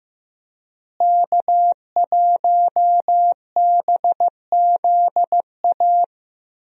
Morse code, heard as a keyed tone, about 15 wpm, 700 Hz. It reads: K1BZA
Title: USA Call Signs - 5 Characters - Repeated 3x - 15wpm